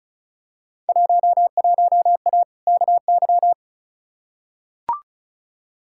Morse code, heard as a keyed tone, about 36 wpm, 700 Hz. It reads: 11A KY
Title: Field Day Exchanges Encore - 35wpm